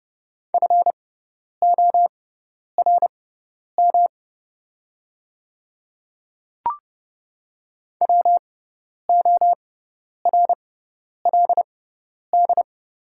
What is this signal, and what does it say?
Morse code, heard as a keyed tone, about 30 wpm, spaced as for 10 wpm, 690 Hz. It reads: FORM WORLD
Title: 6x Character Spacing - Top 300 Words - 30wpm